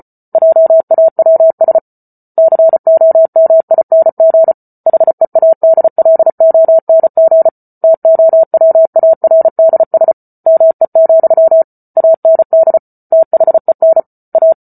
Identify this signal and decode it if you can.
Morse code, heard as a keyed tone, about 34 wpm, 650 Hz. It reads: JAWS COMING HEADLONG TOWARDS ME, AND THEN A